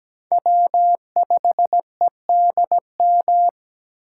Morse code, heard as a keyed tone, about 17 wpm, 700 Hz. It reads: W5EDM
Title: USA Call Signs - 5 Characters - 3x Character Spacing - 17wpm